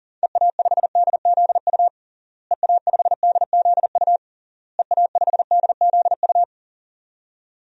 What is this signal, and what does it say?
Morse code, heard as a keyed tone, about 40 wpm, 700 Hz. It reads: EA5DZU EA5DZU EA5DZU
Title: DXCC-100 Call Signs - 6 Characters - Repeated 3x - 40wpm